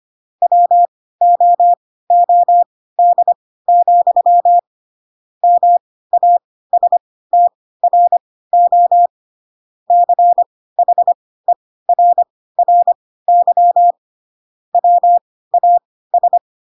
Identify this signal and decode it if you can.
Morse code, heard as a keyed tone, about 25 wpm, spaced as for 16 wpm, 700 Hz. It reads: WOOD, MASTRO CHERRY WAS
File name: Pinocchio / Pinocchio ch1 10 25 ik1hge